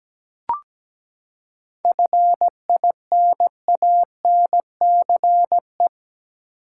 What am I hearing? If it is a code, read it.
Morse code, heard as a keyed tone, about 17 wpm, 700 Hz. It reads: FINANCE
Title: Seven Letter Words - 17wpm